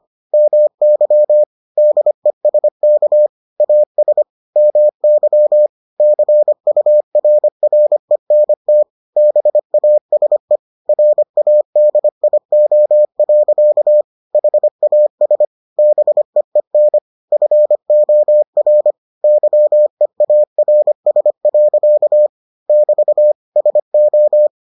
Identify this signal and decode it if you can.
Morse code, heard as a keyed tone, about 25 wpm, 600 Hz. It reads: MY DESK AS MY CURRENT BASE RADIO. HAS BEEN FOR YEARS. <BT> SO